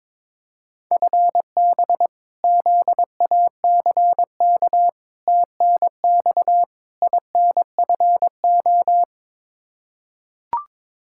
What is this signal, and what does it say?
Morse code, heard as a keyed tone, about 22 wpm, 700 Hz. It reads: FB ZACK TNX INFO
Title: Practice QSO - Exchange 3 - More Info - 22wpm